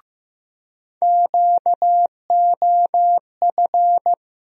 Morse code, heard as a keyed tone, about 15 wpm, 700 Hz. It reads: QOF
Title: USA Call Sign Postfixes - 3 Letters 15wpm